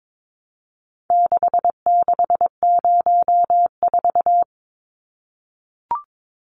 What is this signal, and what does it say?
Morse code, heard as a keyed tone, about 22 wpm, 700 Hz. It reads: 6604
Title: Numbers - 4-Digit 22wpm